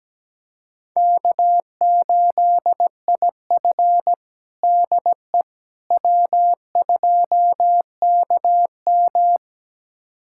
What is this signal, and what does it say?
Morse code, heard as a keyed tone, about 17 wpm, 700 Hz. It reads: K8IF DE W2KM
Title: USA Callsign DE Callsign - 4 Characters - 17wpm